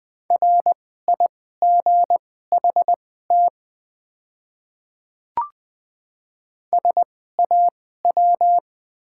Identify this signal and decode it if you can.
Morse code, heard as a keyed tone, about 20 wpm, spaced as for 14 wpm, 690 Hz. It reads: RIGHT SAW
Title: 2x Character Spacing - Top 300 Words - 20wpm